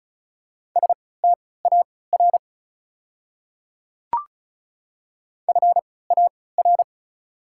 Morse code, heard as a keyed tone, about 36 wpm, spaced as for 20 wpm, 690 Hz. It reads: STAR FAR
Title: 3x Character Spacing - Top 500 Words - 35wpm